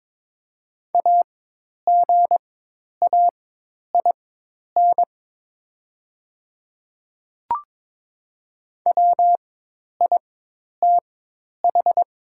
Morse code, heard as a keyed tone, about 22 wpm, spaced as for 10 wpm, 700 Hz. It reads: AGAIN WITH
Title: 4x Character Spacing - Top 300 Words - 22wpm